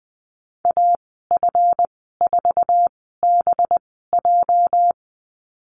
Morse code, heard as a keyed tone, about 20 wpm, spaced as for 15 wpm, 700 Hz. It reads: AF4BJ
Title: 231 Lesson 37g - Letter x - Call Signs 20wpm